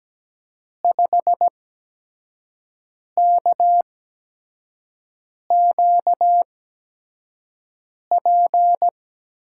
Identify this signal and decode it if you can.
Morse code, heard as a keyed tone, about 17 wpm, spaced as for 5 wpm, 700 Hz. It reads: 5KQP
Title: USA Call Signs - 6 Characters - 8x Character Spacing - 17wpm